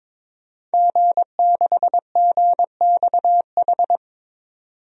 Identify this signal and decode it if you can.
Morse code, heard as a keyed tone, about 22 wpm, 700 Hz. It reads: G6GXH
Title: DXCC-100 Call Signs - 5 Characters - 22wpm